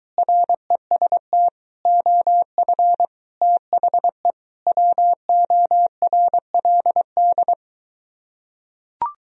Morse code, heard as a keyed tone, about 23 wpm, 700 Hz. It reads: REST OF THE WORLD